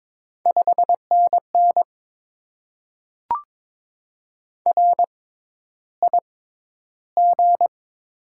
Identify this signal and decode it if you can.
Morse code, heard as a keyed tone, about 22 wpm, spaced as for 11 wpm, 700 Hz. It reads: 5NN RIG
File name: Top 20 QSO Elements - 6x Character Spacing - 22wpm